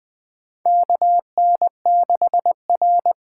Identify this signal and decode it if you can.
Morse code, heard as a keyed tone, about 21 wpm, 700 Hz. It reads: KN6R
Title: USA Call Signs - 4 Characters - 4x Character Spacing - 20wpm